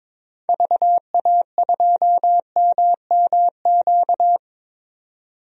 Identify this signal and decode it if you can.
Morse code, heard as a keyed tone, about 22 wpm, 700 Hz. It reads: VA2MMQ
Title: DXCC-100 Call Signs - 6 Characters - 22wpm